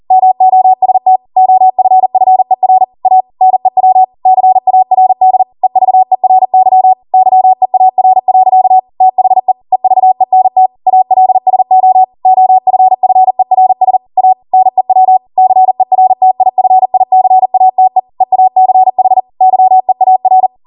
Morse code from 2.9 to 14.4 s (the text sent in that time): A NEW CARD EVERY YEAR. THE EVENT ALSO OFFERS A